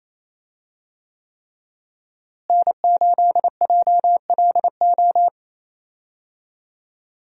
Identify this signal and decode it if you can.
Morse code, heard as a keyed tone, about 28 wpm, 700 Hz. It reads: N8JLO
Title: USA Call Signs - 5 Characters - Repeated 3x - 28wpm